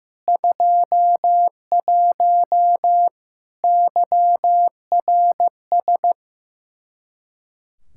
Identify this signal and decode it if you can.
Morse code, heard as a keyed tone, about 15 wpm, 700 Hz. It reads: 21 YRS
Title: Practice QSO - Exchange 3 - More Info - 15wpm